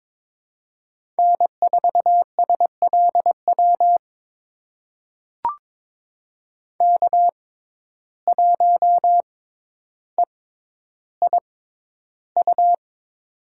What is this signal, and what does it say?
Morse code, heard as a keyed tone, about 22 wpm, spaced as for 11 wpm, 700 Hz. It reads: N4SLW K1EIU
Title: USA Call Signs - 5 Characters - 6x Character Spacing - 22wpm